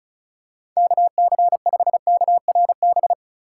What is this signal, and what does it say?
Morse code, heard as a keyed tone, about 36 wpm, 700 Hz. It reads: KC5KRB
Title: USA Call Signs - 6 Characters - 4x Character Spacing - 35wpm